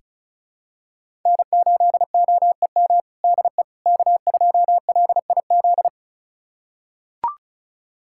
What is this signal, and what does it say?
Morse code, heard as a keyed tone, about 36 wpm, 700 Hz. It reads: N8OEM DE K2LIZ
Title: USA Callsign DE Callsign 35wpm